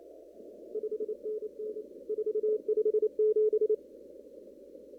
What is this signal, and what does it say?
Morse code, heard as a keyed tone, about 28 wpm, 430 Hz. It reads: 5NN 457
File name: CW / kt5j wpx cw-op-k5pi